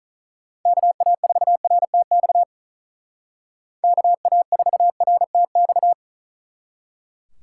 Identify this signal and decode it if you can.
Morse code, heard as a keyed tone, about 44 wpm, 700 Hz. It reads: KA4RTX KA4RTX
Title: Speed Racing - USA 2x3 Call Signs 35wpm